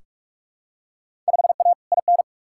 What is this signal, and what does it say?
Morse code, heard as a keyed tone, about 44 wpm, 700 Hz. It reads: 5A IN
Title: Field Day Exchanges Encore - 45wpm